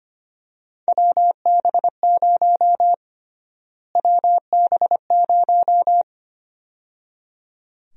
Morse code, heard as a keyed tone, about 25 wpm, 700 Hz. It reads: WB0 WB0
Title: USA Call Sign Prefixes - Repeated 2x - 25wpm